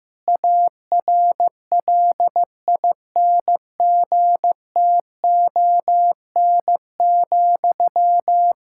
Morse code, heard as a keyed tone, about 15 wpm, 700 Hz. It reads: ARLINGTON,
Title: Practice QSO - Exchange 1 - Beginning - 15wpm